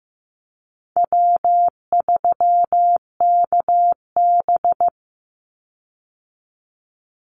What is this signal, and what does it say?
Morse code, heard as a keyed tone, about 15 wpm, 700 Hz. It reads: W3KB
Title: Easy Call Signs Repeated 3x - 15wpm